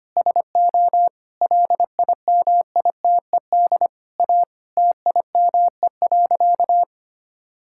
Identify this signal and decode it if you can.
Morse code, heard as a keyed tone, about 25 wpm, 700 Hz. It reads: SO LIMITED A TIME.
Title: Sentences from Top 1000 Words - Review Part 6 - 25wpm